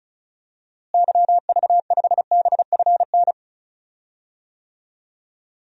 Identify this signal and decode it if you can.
Morse code, heard as a keyed tone, about 36 wpm, 700 Hz. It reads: YV5BFN
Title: DXCC-100 Call Signs - 6 Characters - Repeated 2x - 35wpm